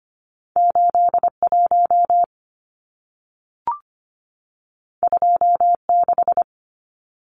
Morse code, heard as a keyed tone, about 25 wpm, 700 Hz. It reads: 81 26
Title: Numbers - 2-Digit 25wpm